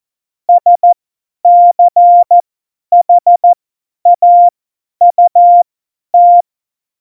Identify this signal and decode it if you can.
Morse code, heard as a keyed tone, about 14 wpm, spaced as for 10 wpm, 700 Hz. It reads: SCHAUT
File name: Morsekurs / Witze / Witz 09. Januar 2018, 50 F70